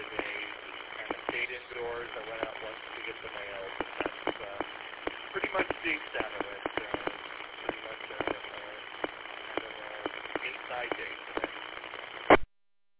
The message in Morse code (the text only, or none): none